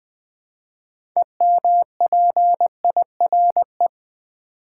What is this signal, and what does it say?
Morse code, heard as a keyed tone, about 20 wpm, 690 Hz. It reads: EMPIRE
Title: Build A Word - 3 Letters At A Time - 6-Letter Words - 20wpm